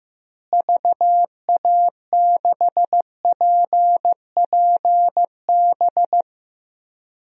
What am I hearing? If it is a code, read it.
Morse code, heard as a keyed tone, about 15 wpm, 700 Hz. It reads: VA6PPB
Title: DXCC-100 Call Signs - 6 Characters - Repeated 2x - 15wpm